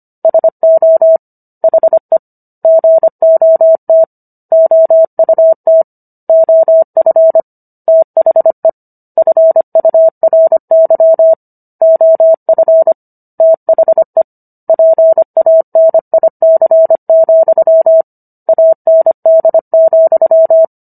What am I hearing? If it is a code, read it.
Morse code, heard as a keyed tone, about 25 wpm, 650 Hz. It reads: SO HE GOT OUT OF THE FURY OF THE PANIC, AND,